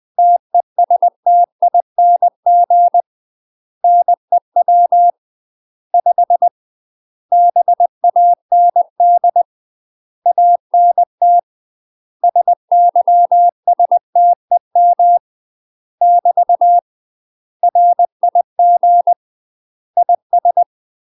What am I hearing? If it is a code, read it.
Morse code, heard as a keyed tone, about 20 wpm, 700 Hz. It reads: TESTING NEW 5 BAND ANT SYSTEM <BT> RIG IS
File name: UK MASTER QSOs TRAINING / 20WPM / UK MASTER QSOs 0015